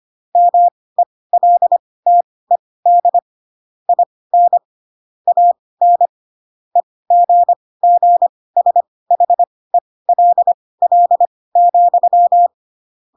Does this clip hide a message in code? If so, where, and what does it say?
Yes: Morse code from 0.4 to 12.5 s, heard as a keyed tone, about 25 wpm, spaced as for 18 wpm, 700 Hz. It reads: MELTED IN AN EGGSHELL,